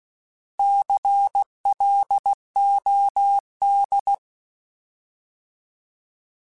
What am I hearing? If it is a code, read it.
Morse code, heard as a keyed tone, about 16 wpm, 800 Hz. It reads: CLOD
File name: Alphabetical / Alphabetical mp3s / Cs @15